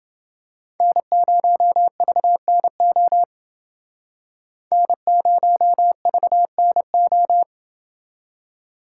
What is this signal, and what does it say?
Morse code, heard as a keyed tone, about 28 wpm, 700 Hz. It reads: N0VNO N0VNO